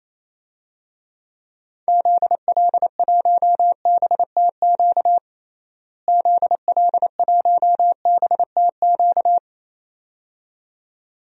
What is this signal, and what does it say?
Morse code, heard as a keyed tone, about 28 wpm, 700 Hz. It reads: ZL1BTQ ZL1BTQ
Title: DXCC-100 Call Signs - 6 Characters - Repeated 2x - 28wpm